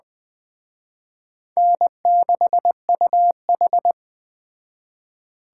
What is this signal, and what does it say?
Morse code, heard as a keyed tone, about 20 wpm, 700 Hz. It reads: N6UH
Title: Speed Racing - USA 1x2 Call Signs 17wpm